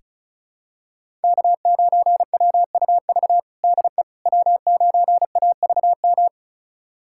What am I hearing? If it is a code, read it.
Morse code, heard as a keyed tone, about 36 wpm, 700 Hz. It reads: K9WUV DE W9AVM
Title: USA Callsign DE Callsign - 5 Characters - 35wpm